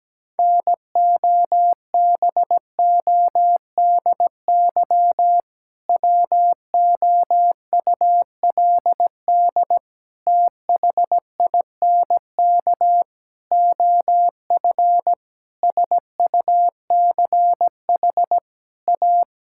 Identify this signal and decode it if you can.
Morse code, heard as a keyed tone, about 17 wpm, 700 Hz. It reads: NOBODY WOULD THINK OF SUCH A
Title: Sentences from Top 1600 Words 17wpm